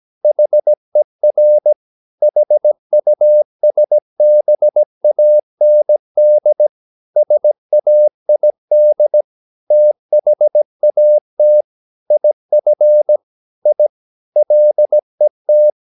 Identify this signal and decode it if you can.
Morse code, heard as a keyed tone, about 17 wpm, 600 Hz. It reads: HER HUSBAND SAID THAT IF I LET
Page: Romanen Never af Ken Follet på engelsk.